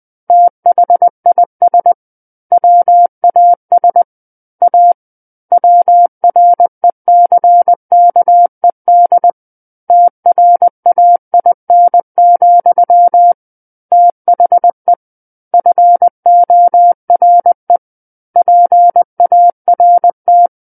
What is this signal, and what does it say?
Morse code, heard as a keyed tone, about 20 wpm, 700 Hz. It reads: THIS WAS A WRECKED TRAIN, THE FORE PART
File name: 20 WPM